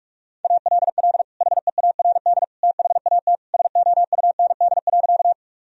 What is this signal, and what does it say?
Morse code, heard as a keyed tone, about 44 wpm, 700 Hz. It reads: ALL HEARD THAT SOUND.
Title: Sentences from Top 1000 Words - Review Part 1 - 45wpm